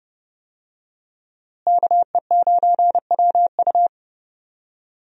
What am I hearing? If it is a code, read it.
Morse code, heard as a keyed tone, about 30 wpm, 700 Hz. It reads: KE9WU
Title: USA Call Signs - 5 Characters - Repeated 2x - 30wpm